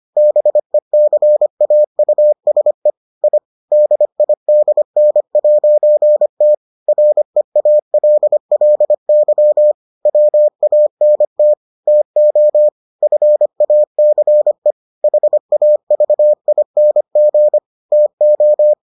Morse code, heard as a keyed tone, about 25 wpm, 600 Hz. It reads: BECAUSE I DIDN'T REALLY WANT TO FACE HAVING TO